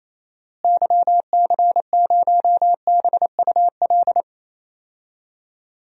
Morse code, heard as a keyed tone, about 28 wpm, 700 Hz. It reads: YC0BUL
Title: DXCC-100 Call Signs - 6 Characters - Repeated 3x - 28wpm